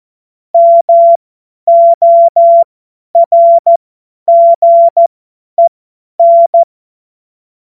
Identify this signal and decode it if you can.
Morse code, heard as a keyed tone, about 14 wpm, spaced as for 10 wpm, 680 Hz. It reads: MORGEN
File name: Morsekurs / Witze / Witz 09. November 2021, 50 F70